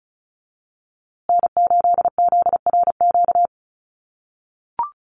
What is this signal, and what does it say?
Morse code, heard as a keyed tone, about 34 wpm, 700 Hz. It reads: N8ZRQ
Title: USA Call Signs Repeated 3x 35wpm